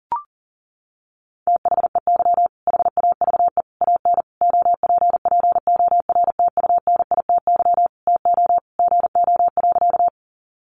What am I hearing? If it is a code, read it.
Morse code, heard as a keyed tone, about 40 wpm, 700 Hz. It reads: THEY HAVE AN OPPORTUNITY TO GO.
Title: Sentences from Top 1000 Words 40wpm